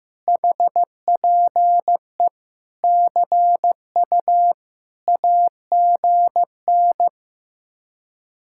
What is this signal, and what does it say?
Morse code, heard as a keyed tone, about 15 wpm, 700 Hz. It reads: HPE CU AGN
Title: Speed-Racing Common QSO Phrases Encore - 15wpm